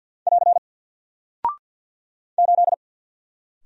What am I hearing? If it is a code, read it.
Morse code, heard as a keyed tone, about 48 wpm, 700 Hz. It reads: <AR> 8
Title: Single Character (Letter:Number:Punctuation:Prosign) Alt 50wpm